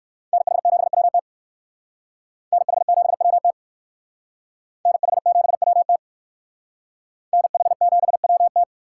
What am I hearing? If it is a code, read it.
Morse code, heard as a keyed tone, about 59 wpm, 700 Hz. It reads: NH7WT NH7WT NH7WT NH7WT